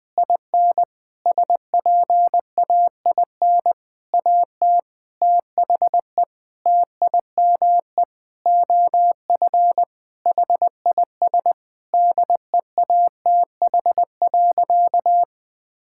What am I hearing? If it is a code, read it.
Morse code, heard as a keyed tone, about 21 wpm, 700 Hz. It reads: IN SPAIN AT THE TIME OF HIS DEATH.
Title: Sentences with 6-Word Phrases - 20wpm